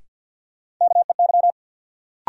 Morse code, heard as a keyed tone, about 52 wpm, 700 Hz. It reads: KEY
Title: Sentences from Top 600 Words - New Word Drill - 50wpm